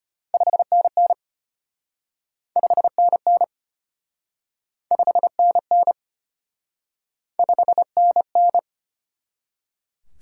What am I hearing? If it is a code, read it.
Morse code, heard as a keyed tone, about 32 wpm, 700 Hz. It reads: 5NN 5NN 5NN 5NN